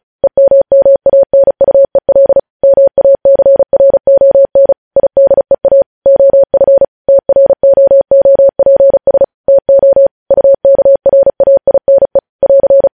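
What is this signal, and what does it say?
Morse code, heard as a keyed tone, about 36 wpm, 550 Hz. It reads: EMMANUEL MACRON IDEA OF TROOPS TO UKRAINE <AR>
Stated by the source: Morse code news headlines at 35.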